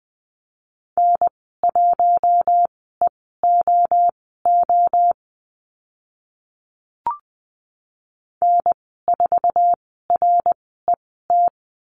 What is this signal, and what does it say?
Morse code, heard as a keyed tone, about 20 wpm, spaced as for 14 wpm, 700 Hz. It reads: N1EOO N4RET
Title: 069 Lesson 12g - Letter d - Call Signs 20wpm